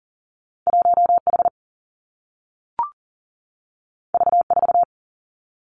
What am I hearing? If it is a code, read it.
Morse code, heard as a keyed tone, about 40 wpm, 700 Hz. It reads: 1 H V 4
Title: Two Challenging Letter-Number Combinations 40wpm